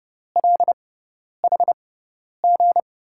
Morse code, heard as a keyed tone, about 30 wpm, spaced as for 10 wpm, 700 Hz. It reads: LHG
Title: USA Call Signs - 5 Characters - 6x Character Spacing - 30wpm